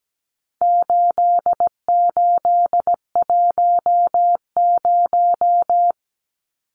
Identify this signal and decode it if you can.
Morse code, heard as a keyed tone, about 17 wpm, 700 Hz. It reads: 8810
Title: Numbers - 4-Digit 17wpm